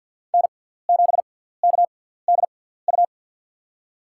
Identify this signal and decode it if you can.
Morse code, heard as a keyed tone, about 48 wpm, spaced as for 17 wpm, 700 Hz. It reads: N7KDU
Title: USA Call Signs - 5 Characters - 6x Character Spacing - 50wpm